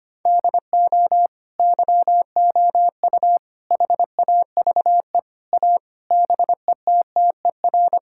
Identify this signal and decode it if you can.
Morse code, heard as a keyed tone, about 25 wpm, 700 Hz. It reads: DO YOU HAVE A BETTER
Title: Sentences from Top 1000 Words - Review Part 5 - 25wpm